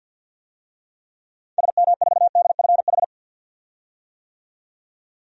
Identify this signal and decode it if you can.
Morse code, heard as a keyed tone, about 52 wpm, 700 Hz. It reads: SM4DFH
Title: DXCC-100 Call Signs - 6 Characters - 50wpm